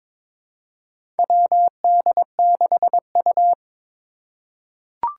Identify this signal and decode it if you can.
Morse code, heard as a keyed tone, about 22 wpm, 700 Hz. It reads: WD6U
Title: USA Call Signs - 4 Characters - 3x Character Spacing - 22wpm